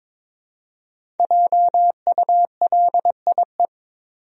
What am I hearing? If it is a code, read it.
Morse code, heard as a keyed tone, about 21 wpm, 700 Hz. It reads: JULIE
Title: USA Common Names - Max 5 Letters - 22wpm